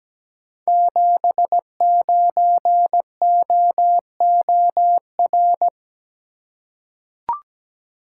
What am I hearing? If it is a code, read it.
Morse code, heard as a keyed tone, about 17 wpm, 700 Hz. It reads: 79OOR
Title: Code Groups - 5 Letter:Numbers - 17wpm